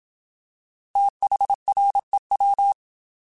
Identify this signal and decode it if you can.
Morse code, heard as a keyed tone, about 27 wpm, 800 Hz. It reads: THREW
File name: TH / TH mp3s / Fast TH @25